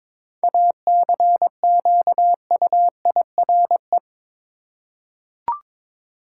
Morse code, heard as a keyed tone, about 21 wpm, 700 Hz. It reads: ACQUIRE
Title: Seven Letter Words - 22wpm